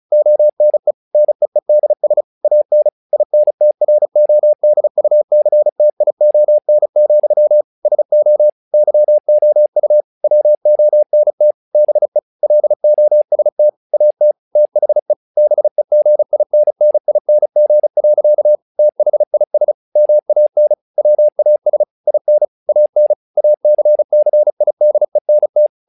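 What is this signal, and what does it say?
Morse code, heard as a keyed tone, about 35 wpm, 600 Hz. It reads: ONE NEEDS AN INTRODUCTION, SO YOU WONT BE LOST AT THE BEGINNING. THIS MAN WAS IN AN ACCIDENT
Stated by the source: Morse practice text "Tower Trouble" sent at 35 WPM, from the SKCC - Straight Key Century Club learning center audio files.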